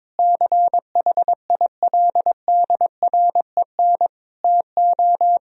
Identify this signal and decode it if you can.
Morse code, heard as a keyed tone, about 21 wpm, 700 Hz. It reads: CHILDREN TO
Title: Sentences from Top 400 Words and Abbreviations 22wpm